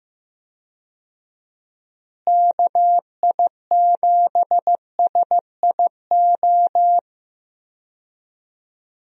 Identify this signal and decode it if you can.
Morse code, heard as a keyed tone, about 16 wpm, 700 Hz. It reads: KI7SIO
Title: USA Call Signs - 6 Characters - Repeated 2x - 15wpm